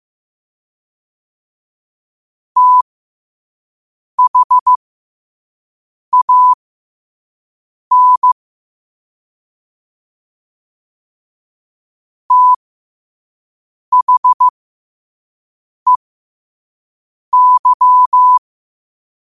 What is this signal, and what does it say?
Morse code, heard as a keyed tone, about 15 wpm, spaced as for 5 wpm, 1000 Hz. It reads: THAN THEY
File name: qotdfiles / 5wpm